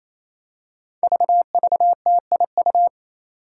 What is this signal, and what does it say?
Morse code, heard as a keyed tone, about 27 wpm, 700 Hz. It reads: VVTIU
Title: Five Challenging Letter-Number Combinations 28wpm